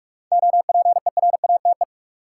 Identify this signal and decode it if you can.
Morse code, heard as a keyed tone, about 46 wpm, 700 Hz. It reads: OPERATE
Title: Sentences from Top 1500 Words - New Word Drill - 45wpm